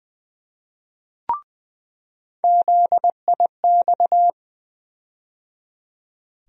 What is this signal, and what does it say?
Morse code, heard as a keyed tone, about 20 wpm, 700 Hz. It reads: ZIX
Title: USA Call Sign Postfixes - 3 Letters 20wpm